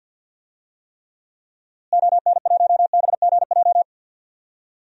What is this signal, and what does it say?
Morse code, heard as a keyed tone, about 48 wpm, 700 Hz. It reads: ON1BGJ
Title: DXCC-100 Call Signs - 6 Characters - Repeated 2x - 50wpm